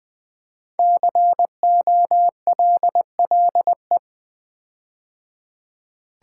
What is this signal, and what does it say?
Morse code, heard as a keyed tone, about 20 wpm, 700 Hz. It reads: COLLE
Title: Build A Word - 5 Letters At A Time - 10-Letter Words - 20wpm